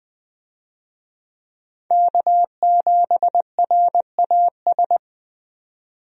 Morse code, heard as a keyed tone, about 21 wpm, 700 Hz. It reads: K7RAS
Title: USA Call Signs - 5 Characters - 20wpm